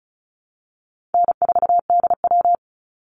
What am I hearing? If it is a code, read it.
Morse code, heard as a keyed tone, about 34 wpm, 700 Hz. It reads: N4DW
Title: Call Signs Repeated 3x - 35wpm